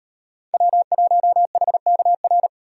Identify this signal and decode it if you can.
Morse code, heard as a keyed tone, about 38 wpm, 700 Hz. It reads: W1HKR